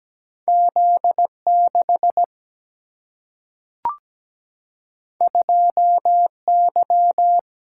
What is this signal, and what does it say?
Morse code, heard as a keyed tone, about 17 wpm, 700 Hz. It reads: Z6 2Y
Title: Single-Letter Single-Number 17wpm